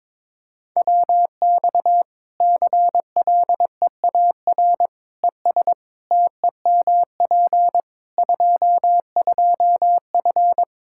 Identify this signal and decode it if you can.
Morse code, heard as a keyed tone, about 22 wpm, 700 Hz. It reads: WX CLEAR ES TEMP 22F
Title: Practice QSO - Exchange 2 - Info - 22wpm